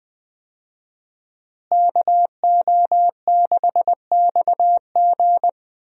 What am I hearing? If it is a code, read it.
Morse code, heard as a keyed tone, about 20 wpm, 700 Hz. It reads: KO6XG
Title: USA Call Signs - 5 Characters - Repeated 3x - 20wpm